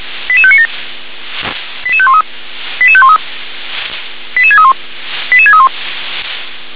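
Five-tone selective call (ZVEI), heard as quick runs of five tones, tones of about 70 ms. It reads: 88597 88513 88513 88631 88631
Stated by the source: ZVEI five-tone selective call, received by SDR.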